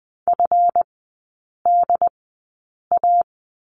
Morse code, heard as a keyed tone, about 19 wpm, spaced as for 8 wpm, 700 Hz. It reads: FDA
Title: 138 Lesson 23e - Letter f - Three Characters 20wpm